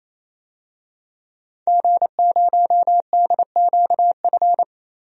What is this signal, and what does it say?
Morse code, heard as a keyed tone, about 28 wpm, 700 Hz. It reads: G0DQF
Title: DXCC-100 Call Signs - 5 Characters - Repeated 3x - 28wpm